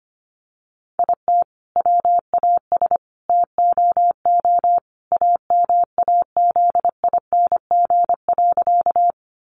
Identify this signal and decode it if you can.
Morse code, heard as a keyed tone, about 25 wpm, 700 Hz. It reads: IT WAS TOO AMAZING.
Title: Sentences from Top 1500 Words 25wpm